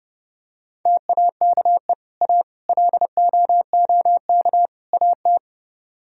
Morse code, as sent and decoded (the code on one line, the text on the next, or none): - .- -.- . / .- / .-.. --- --- -.- / .- -
TAKE A LOOK AT